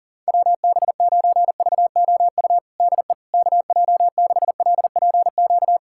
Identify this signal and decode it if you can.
Morse code, heard as a keyed tone, about 40 wpm, 700 Hz. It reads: WB9VOU DE KJ6LPQ
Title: USA Callsign DE Callsign - 6 Characters - 40wpm